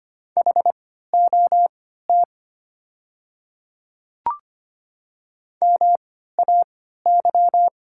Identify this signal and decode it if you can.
Morse code, heard as a keyed tone, about 25 wpm, spaced as for 14 wpm, 700 Hz. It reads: HOT MAY
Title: 3x Character Spacing - Top 100 Words - 25wpm